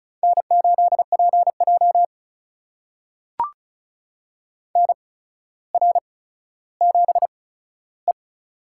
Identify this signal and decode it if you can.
Morse code, heard as a keyed tone, about 36 wpm, spaced as for 15 wpm, 700 Hz. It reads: N8PJ NR7E
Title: USA Call Signs - 4 Characters - 8x Character Spacing - 35wpm